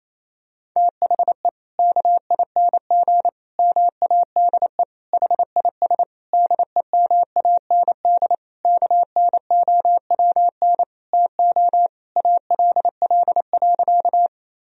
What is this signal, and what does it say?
Morse code, heard as a keyed tone, about 28 wpm, 700 Hz. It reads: THE KING MADE HIS DEMAND KNOWN TO ALL.
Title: Sentences from Top 1000 Words - Review Part 4 - 28wpm